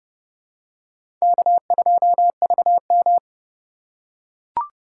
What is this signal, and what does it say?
Morse code, heard as a keyed tone, about 31 wpm, 700 Hz. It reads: K2VM
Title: USA Call Signs - 4 Characters - 4x Character Spacing - 30wpm